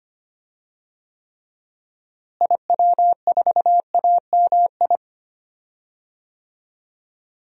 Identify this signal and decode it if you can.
Morse code, heard as a keyed tone, about 25 wpm, 700 Hz. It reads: IW4AMI
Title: DXCC-100 Call Signs - 6 Characters - Repeated 3x - 25wpm